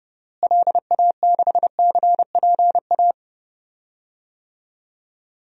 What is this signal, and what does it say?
Morse code, heard as a keyed tone, about 30 wpm, 700 Hz. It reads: LA6CPA
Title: DXCC-100 Call Signs - 6 Characters - Repeated 2x - 30wpm